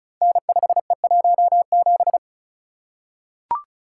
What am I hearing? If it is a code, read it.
Morse code, heard as a keyed tone, about 36 wpm, 700 Hz. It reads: N5E17
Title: Code Groups - 5 Letter:Numbers - 35wpm